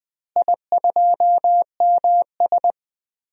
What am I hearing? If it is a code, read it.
Morse code, heard as a keyed tone, about 20 wpm, 700 Hz. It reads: I2MS
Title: DXCC-100 Call Signs - 4 Characters - Repeated 2x - 20wpm